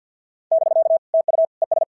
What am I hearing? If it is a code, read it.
Morse code, heard as a keyed tone, about 50 wpm, 650 Hz. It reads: <BK> TU ES
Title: SOTA QSO Practice 50wpm